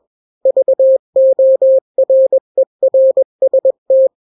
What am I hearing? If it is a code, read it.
Morse code, heard as a keyed tone, about 21 wpm, 530 Hz. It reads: VORERST
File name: daily / 20wpm